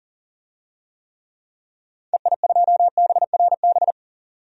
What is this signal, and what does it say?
Morse code, heard as a keyed tone, about 37 wpm, 700 Hz. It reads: EI2BRB
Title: DXCC-100 Call Signs - 6 Characters - 40wpm